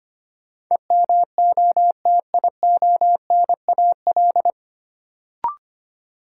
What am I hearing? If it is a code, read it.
Morse code, heard as a keyed tone, about 25 wpm, 700 Hz. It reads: EMOTIONAL
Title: Sentences from Top 1000 Words - Review Part 5 - New Word Drill - 25wpm